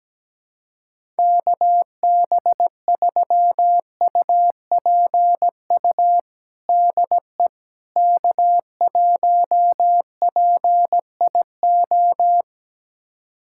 Morse code, heard as a keyed tone, about 17 wpm, 700 Hz. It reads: KB3UPU DE K1PIO
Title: USA Callsign DE Callsign 17wpm